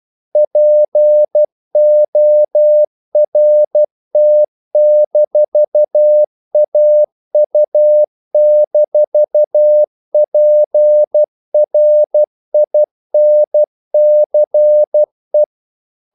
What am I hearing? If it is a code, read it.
Morse code, heard as a keyed tone, about 12 wpm, 600 Hz. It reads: PORT-AU-PRINCE
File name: CW / Practice / NationalCapitals12